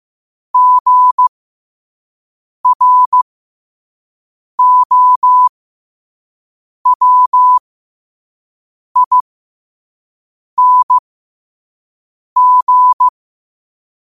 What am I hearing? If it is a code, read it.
Morse code, heard as a keyed tone, about 15 wpm, spaced as for 5 wpm, 1000 Hz. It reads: GROWING
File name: qotdfiles / 5wpm